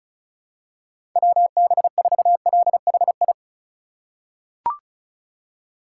Morse code, heard as a keyed tone, about 36 wpm, 700 Hz. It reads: WB4LHI
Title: USA Call Signs - 6 Characters - 3x Character Spacing - 35wpm